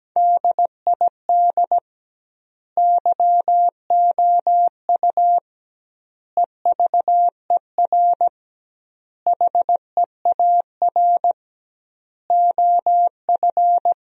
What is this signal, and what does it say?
Morse code, heard as a keyed tone, about 17 wpm, 700 Hz. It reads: DID YOU EVER HEAR OF
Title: 2x Word-Spacing - Sentences from Top 400 Words - 17wpm